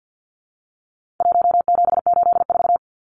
Morse code, heard as a keyed tone, about 50 wpm, 700 Hz. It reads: 1784
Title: Numbers - 4-Digit 50wpm